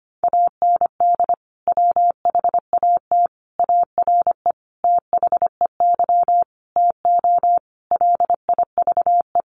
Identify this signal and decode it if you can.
Morse code, heard as a keyed tone, about 25 wpm, 700 Hz. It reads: AND WHAT ARE THEY TO LIVE
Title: Sentences from Top 1100 Words 25wpm